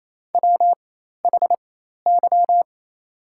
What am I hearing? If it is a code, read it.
Morse code, heard as a keyed tone, about 27 wpm, spaced as for 13 wpm, 700 Hz. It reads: WHY
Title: 4x Character Spacing - Top 200 Words - 28wpm